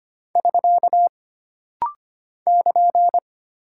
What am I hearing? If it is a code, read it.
Morse code, heard as a keyed tone, about 25 wpm, 700 Hz. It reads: <SK> <KN>
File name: Single Punctuation-Prosign - Rapid-Fire - 25wpm